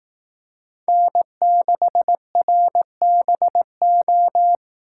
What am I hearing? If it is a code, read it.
Morse code, heard as a keyed tone, about 18 wpm, 700 Hz. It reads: N6RBO
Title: Speed Racing - USA Call Signs 15wpm